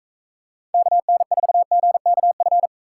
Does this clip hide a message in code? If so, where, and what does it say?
Yes: Morse code from 0.7 to 2.7 s, heard as a keyed tone, about 44 wpm, 700 Hz. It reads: KN4GKF